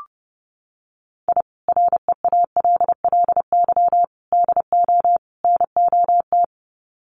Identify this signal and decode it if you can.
Morse code, heard as a keyed tone, about 30 wpm, 700 Hz. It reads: I REALLY DO NOT
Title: Sets of 4 Words 30wpm